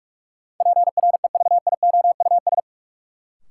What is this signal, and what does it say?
Morse code, heard as a keyed tone, about 46 wpm, 700 Hz. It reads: PREVIOUS
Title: Sentences from Top 1500 Words - New Word Drill Alt - 45wpm